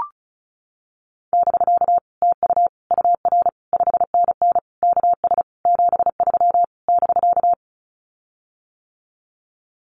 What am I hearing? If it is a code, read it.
Morse code, heard as a keyed tone, about 34 wpm, 700 Hz. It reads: <BK> TU UR 5NN KS 73 <BK>
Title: Common QSO Phrases Encore 35wpm